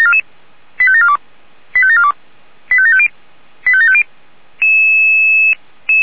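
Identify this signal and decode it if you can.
Five-tone selective call (ZVEI), heard as quick runs of five tones, tones of about 70 ms. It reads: 85742 85742 85750 85750
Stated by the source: ZVEI five-tone selective call received on an SDR.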